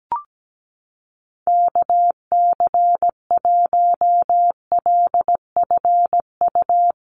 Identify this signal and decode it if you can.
Morse code, heard as a keyed tone, about 17 wpm, 700 Hz. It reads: KC1LFU
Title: USA Call Signs Repeated 2x 17wpm